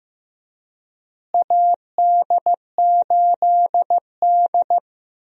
Morse code, heard as a keyed tone, about 16 wpm, 700 Hz. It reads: AD8D
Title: USA Call Signs - 4 Characters - Repeated 2x - 15wpm